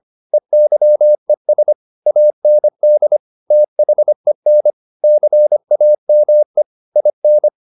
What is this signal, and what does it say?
Morse code, heard as a keyed tone, about 25 wpm, 600 Hz. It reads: EYES AND THEN CAME IN